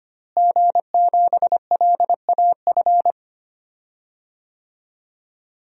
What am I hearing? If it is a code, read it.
Morse code, heard as a keyed tone, about 25 wpm, 700 Hz. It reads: G7LAF
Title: DXCC-100 Call Signs - 5 Characters - Repeated 2x - 25wpm